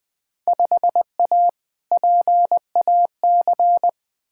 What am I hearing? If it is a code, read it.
Morse code, heard as a keyed tone, about 20 wpm, 700 Hz. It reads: 5A PAC
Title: Field Day Exchanges Encore - 20wpm